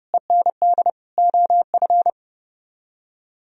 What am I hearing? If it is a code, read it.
Morse code, heard as a keyed tone, about 34 wpm, 700 Hz. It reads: END OF